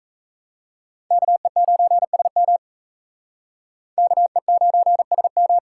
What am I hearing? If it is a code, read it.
Morse code, heard as a keyed tone, about 37 wpm, 700 Hz. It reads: KE9SM KE9SM